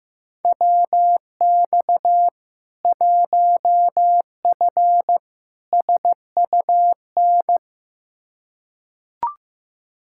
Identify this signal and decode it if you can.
Morse code, heard as a keyed tone, about 15 wpm, 700 Hz. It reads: WX 1F SUN
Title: Practice QSO - Exchange 2 - Info - 15wpm